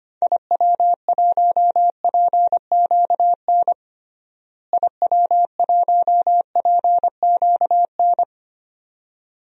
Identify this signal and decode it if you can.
Morse code, heard as a keyed tone, about 25 wpm, 700 Hz. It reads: IW1PQN IW1PQN
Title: DXCC-100 Call Signs - 6 Characters - Repeated 2x - 25wpm